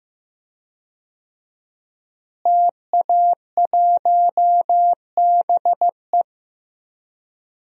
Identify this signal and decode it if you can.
Morse code, heard as a keyed tone, about 15 wpm, 700 Hz. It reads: TA1BE
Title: DXCC-100 Call Signs - 5 Characters - Repeated 3x - 15wpm